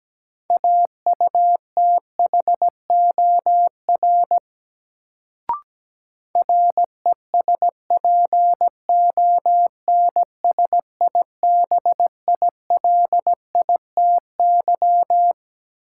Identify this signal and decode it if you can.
Morse code, heard as a keyed tone, about 17 wpm, 700 Hz. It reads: AUTHOR RESPONSIBILITY
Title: Sentences from Top 1200 Words - New Word Drill - 17wpm